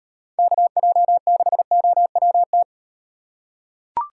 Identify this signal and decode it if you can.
Morse code, heard as a keyed tone, about 38 wpm, 700 Hz. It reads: KJ6OWT